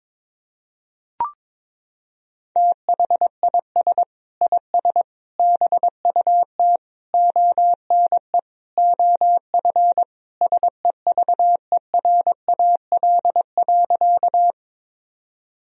Morse code, heard as a keyed tone, about 22 wpm, 700 Hz. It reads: THIS IS BUT ONE OF SEVERAL.
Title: Sentences from Top 1000 Words - Review Part 2 - 22wpm